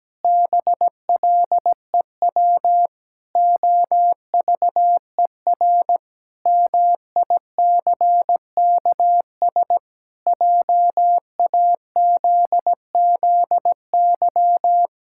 Morse code, heard as a keyed tone, about 17 wpm, 700 Hz. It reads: BLEW OVER MICKS JAZZY
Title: Pangram Challenge v2 - 17wpm